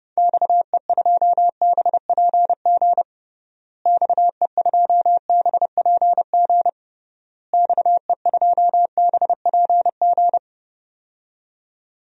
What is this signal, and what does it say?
Morse code, heard as a keyed tone, about 30 wpm, 700 Hz. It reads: XE2BPG XE2BPG XE2BPG
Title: DXCC-100 Call Signs - 6 Characters - Repeated 3x - 30wpm